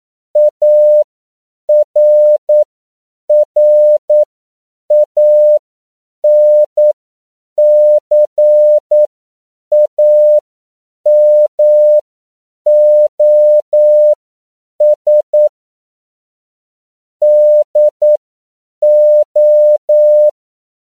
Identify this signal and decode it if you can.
Morse code, heard as a keyed tone, about 9 wpm, spaced as for 7 wpm, 600 Hz. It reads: ARRANCAMOS DO
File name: CW / 10PPM